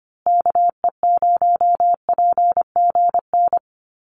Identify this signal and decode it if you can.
Morse code, heard as a keyed tone, about 25 wpm, 700 Hz. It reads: KE0PGN
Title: USA Call Signs Repeated 3x 25wpm